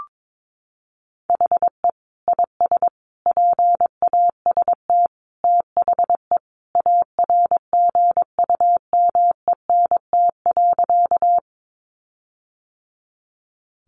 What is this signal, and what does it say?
Morse code, heard as a keyed tone, about 22 wpm, 700 Hz. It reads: HE IS PAST THE ARGUMENT.
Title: Sentences from Top 1400 Words 22wpm